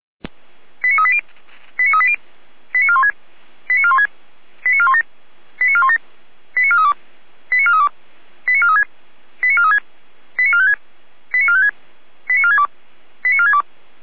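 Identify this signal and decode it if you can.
Five-tone selective call (ZVEI), heard as quick runs of five tones, tones of about 70 ms. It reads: 89380 89380 89416 89416 89417 89417 89432 89432 89436 89436 89456 89456 89462 89462
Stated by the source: ZVEI five-tone selective call 89361, received by SDR, from the audio archive of Feuerwehr Orlamünde.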